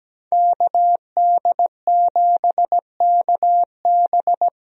Morse code, heard as a keyed tone, about 17 wpm, 700 Hz. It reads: KD7KB
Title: USA Call Signs - 5 Characters - Repeated 2x - 17wpm